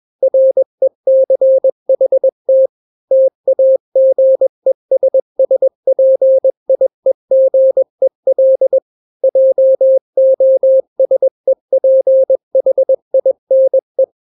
Morse code, heard as a keyed tone, about 21 wpm, 530 Hz. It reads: RECHT TAGESSPIEGEL JOSEPHINE
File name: daily / 20wpm